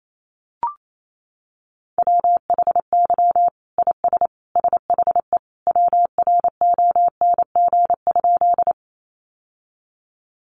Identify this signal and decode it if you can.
Morse code, heard as a keyed tone, about 28 wpm, 700 Hz. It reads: WHY IS SHE WRONG?
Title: Sentences from Top 900 Words 28wpm